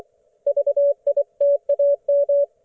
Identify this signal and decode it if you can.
Morse code, heard as a keyed tone, about 24 wpm, 570 Hz. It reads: VITAM
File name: Archiv / CUC224 / ok1fhi cul224